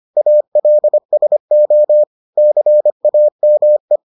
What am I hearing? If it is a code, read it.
Morse code, heard as a keyed tone, about 25 wpm, 600 Hz. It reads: ALSO CAME